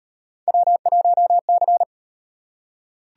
Morse code, heard as a keyed tone, about 38 wpm, 700 Hz. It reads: W1C